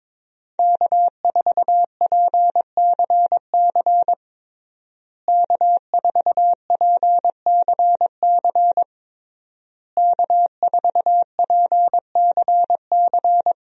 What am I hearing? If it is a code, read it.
Morse code, heard as a keyed tone, about 21 wpm, 700 Hz. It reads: K4PCC K4PCC K4PCC
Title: USA Call Signs - 5 Characters - Repeated 3x - 22wpm